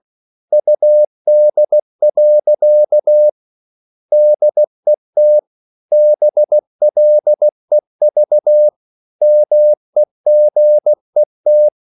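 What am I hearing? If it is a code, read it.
Morse code, heard as a keyed tone, about 16 wpm, 600 Hz. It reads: UD. DET BLEV MEGET
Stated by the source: Slow (QRS) Morse code at 16 wpm, from CW med Gnister.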